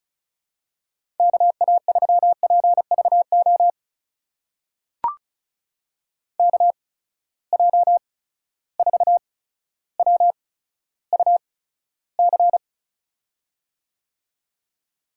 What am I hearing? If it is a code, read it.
Morse code, heard as a keyed tone, about 36 wpm, spaced as for 15 wpm, 700 Hz. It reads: KA3PVO KJ4WUC
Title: USA Call Signs - 6 Characters - 8x Character Spacing - 35wpm